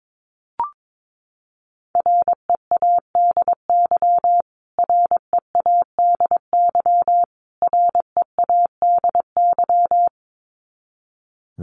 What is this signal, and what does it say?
Morse code, heard as a keyed tone, about 22 wpm, 700 Hz. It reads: READY READY READY
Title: Top 500 Words Repeated 22wpm